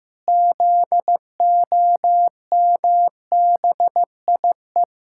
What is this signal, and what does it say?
Morse code, heard as a keyed tone, about 15 wpm, 700 Hz. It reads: ZOMBIE
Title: Pangram Challenge v2 - 15wpm